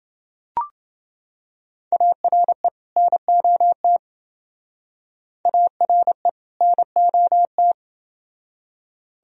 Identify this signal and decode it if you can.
Morse code, heard as a keyed tone, about 29 wpm, 700 Hz. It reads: ARE NOT ARE NOT